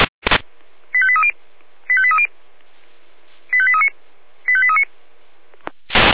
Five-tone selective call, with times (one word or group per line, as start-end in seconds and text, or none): 0.9-1.3 85830
1.9-2.3 85830
3.5-3.9 85839
4.5-4.8 85839